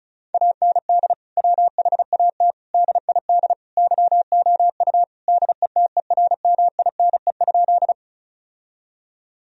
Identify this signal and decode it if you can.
Morse code, heard as a keyed tone, about 34 wpm, 700 Hz. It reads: AND WHAT DID YOU DETERMINE?
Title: Sentences from Top 1000 Words - Review Part 4 - 35wpm